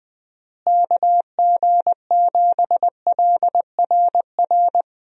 Morse code, heard as a keyed tone, about 20 wpm, 700 Hz. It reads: KG7LRR
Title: USA Call Signs - 6 Characters - Repeated 3x - 20wpm